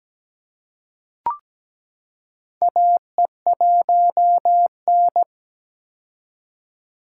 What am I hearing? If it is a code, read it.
Morse code, heard as a keyed tone, about 17 wpm, 700 Hz. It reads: AE1N
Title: K1USN SST QSO Practice - 17wpm